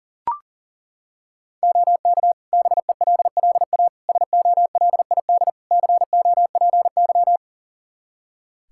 Morse code, heard as a keyed tone, about 40 wpm, 700 Hz. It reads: OK BELLA SOLID COPY
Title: K1USN SST QSO Practice 40wpm